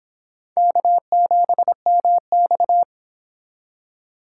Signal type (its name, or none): Morse code